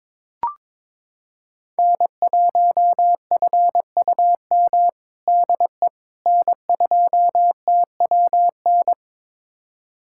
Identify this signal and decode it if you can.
Morse code, heard as a keyed tone, about 21 wpm, 700 Hz. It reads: N1FUM DE N2TWN
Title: USA Callsign DE Callsign - 5 Characters - 22wpm